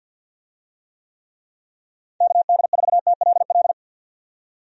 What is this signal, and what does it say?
Morse code, heard as a keyed tone, about 50 wpm, 700 Hz. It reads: KD4TLL
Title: USA Call Signs - 6 Characters - 50wpm